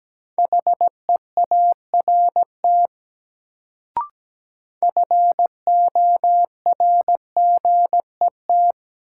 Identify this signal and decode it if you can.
Morse code, heard as a keyed tone, about 17 wpm, 700 Hz. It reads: HEART FORGET
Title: Sentences from Top 1200 Words - New Word Drill Alt - 17wpm